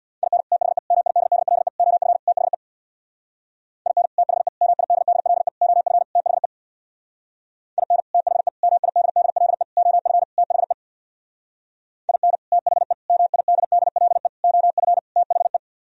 Morse code, heard as a keyed tone, about 62 wpm, 700 Hz. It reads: IN THE MIDDLE OF THE IN THE MIDDLE OF THE IN THE MIDDLE OF THE IN THE MIDDLE OF THE
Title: Speed-Racing - Sets of 5 Words 50wpm